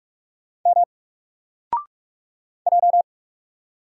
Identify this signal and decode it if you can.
Morse code, heard as a keyed tone, about 44 wpm, 700 Hz. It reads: M J
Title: Single Character (Letter:Number:Punctuation:Prosign) Alt 45wpm